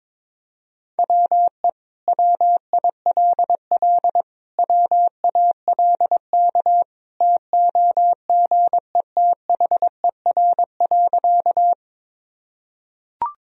Morse code, heard as a keyed tone, about 22 wpm, 700 Hz. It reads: WE WILL WALK TOGETHER.
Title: Sentences from Top 1000 Words - Review Part 1 - 22wpm